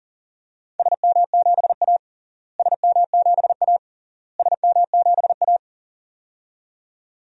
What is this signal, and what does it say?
Morse code, heard as a keyed tone, about 40 wpm, 700 Hz. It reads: SM7A SM7A SM7A
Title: DXCC-100 Call Signs - 4 Characters - Repeated 3x - 40wpm